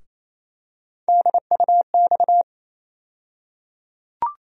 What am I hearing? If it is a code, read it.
Morse code, heard as a keyed tone, about 28 wpm, 700 Hz. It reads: DUX
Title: USA Call Sign Postfixes - 3 Letters 28wpm